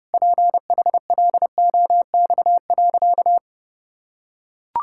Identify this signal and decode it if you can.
Morse code, heard as a keyed tone, about 30 wpm, 700 Hz. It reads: PHLOX.
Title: Pangram Challenge v2 - 30wpm